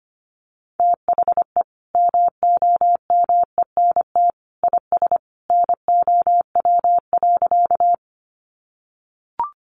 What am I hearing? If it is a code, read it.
Morse code, heard as a keyed tone, about 25 wpm, 700 Hz. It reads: THE MOMENT IS NOW.
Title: Sentences from Top 1000 Words 25wpm